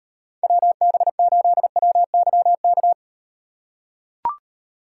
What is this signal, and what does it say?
Morse code, heard as a keyed tone, about 38 wpm, 700 Hz. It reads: WB8WYK